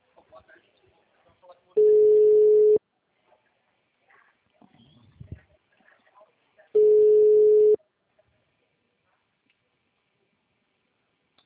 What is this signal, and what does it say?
Ringback tone (Europe, 425 Hz), with speech.